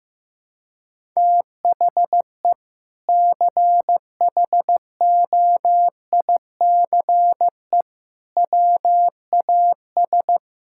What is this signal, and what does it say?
Morse code, heard as a keyed tone, about 15 wpm, 700 Hz. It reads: THE CHOICE WAS
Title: Sentences from Top 1000 Words - Review Part 6 - 15wpm